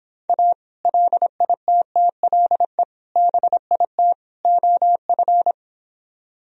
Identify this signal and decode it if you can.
Morse code, heard as a keyed tone, about 26 wpm, 700 Hz. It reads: A LITTLE BIT OF